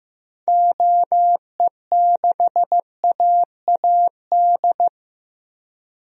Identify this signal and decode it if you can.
Morse code, heard as a keyed tone, about 15 wpm, 700 Hz. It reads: OE6AAD
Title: DXCC-100 Call Signs - 6 Characters - 15wpm